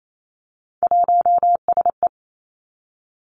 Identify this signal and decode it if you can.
Morse code, heard as a keyed tone, about 28 wpm, 700 Hz. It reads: 1SE
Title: Three Challenging Letter-Number Combinations 28wpm